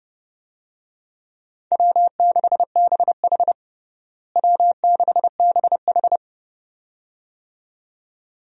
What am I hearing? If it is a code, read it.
Morse code, heard as a keyed tone, about 30 wpm, 700 Hz. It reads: W6BH W6BH
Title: USA Call Signs - 4 Characters - Repeated 2x - 30wpm